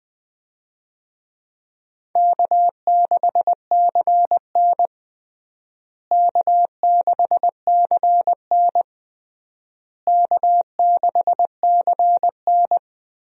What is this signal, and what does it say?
Morse code, heard as a keyed tone, about 21 wpm, 700 Hz. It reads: K6CN K6CN K6CN
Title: USA Call Signs - 4 Characters - Repeated 3x - 20wpm